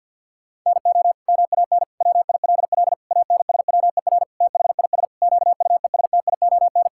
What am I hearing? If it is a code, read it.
Morse code, heard as a keyed tone, about 50 wpm, 700 Hz. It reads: NO MAN WILL ANSWER THIS QUESTION
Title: Sentences from Top 1000 Words - Review Part 6 - 50wpm